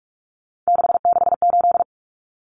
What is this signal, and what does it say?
Morse code, heard as a keyed tone, about 44 wpm, 700 Hz. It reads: 668
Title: Common Cut Numbers - 3-Digit 45wpm